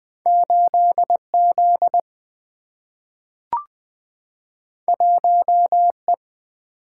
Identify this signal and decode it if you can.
Morse code, heard as a keyed tone, about 20 wpm, 700 Hz. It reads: 8Z 1E
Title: Single-Letter Single-Number 20wpm